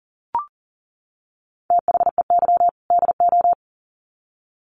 Morse code, heard as a keyed tone, about 40 wpm, 700 Hz. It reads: THEY DO
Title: Sets of 2 Words 40wpm